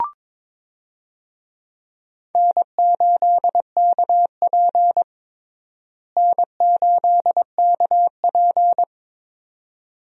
Morse code, heard as a keyed tone, about 22 wpm, 700 Hz. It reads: N8KP N8KP
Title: USA Call Signs - 4 Characters - Repeated 2x - 22wpm